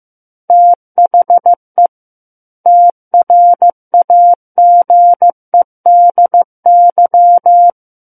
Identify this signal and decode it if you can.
Morse code, heard as a keyed tone, about 15 wpm, 700 Hz. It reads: THE TRAGEDY
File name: Book One / 06-Chapter 6 - 15 WPM